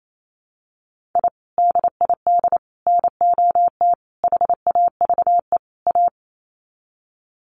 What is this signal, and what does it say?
Morse code, heard as a keyed tone, about 28 wpm, 700 Hz. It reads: I DID NOT HAVE A
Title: Sets of 5 Words 28wpm